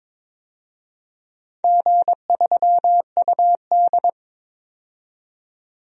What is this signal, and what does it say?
Morse code, heard as a keyed tone, about 22 wpm, 700 Hz. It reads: G3UD
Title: DXCC-100 Call Signs - 4 Characters - 22wpm